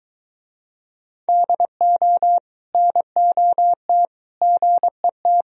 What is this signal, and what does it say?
Morse code, heard as a keyed tone, about 22 wpm, 700 Hz. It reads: DO NOT GET